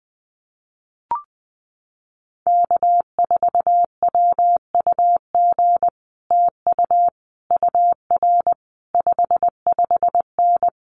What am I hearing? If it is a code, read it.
Morse code, heard as a keyed tone, about 19 wpm, 700 Hz. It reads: K4WUG TU UR 55N
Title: Common QSO Phrases Encore 20wpm